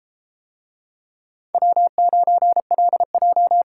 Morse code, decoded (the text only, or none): W9LJ